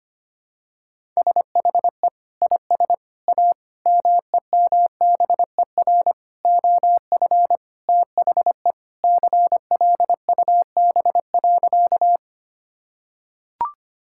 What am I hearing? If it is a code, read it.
Morse code, heard as a keyed tone, about 25 wpm, 700 Hz. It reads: SHE IS A MEMBER OF THE CLUB.
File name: practice / 25wpm / Sentences with 4-Word Phrases - 25wpm